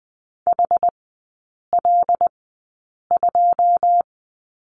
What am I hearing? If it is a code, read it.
Morse code, heard as a keyed tone, about 20 wpm, spaced as for 8 wpm, 700 Hz. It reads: HL2
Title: 081 Lesson 14e - Number 2 - Three Characters 20wpm